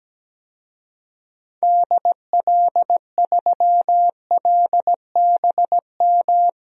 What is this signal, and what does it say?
Morse code, heard as a keyed tone, about 17 wpm, 700 Hz. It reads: DL3LBM
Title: DXCC-100 Call Signs - 6 Characters - Repeated 3x - 17wpm